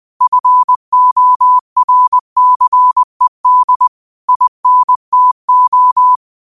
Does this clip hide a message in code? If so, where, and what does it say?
Yes: Morse code from 0.2 to 6.2 s, heard as a keyed tone, about 20 wpm, 1000 Hz. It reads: FORCED INTO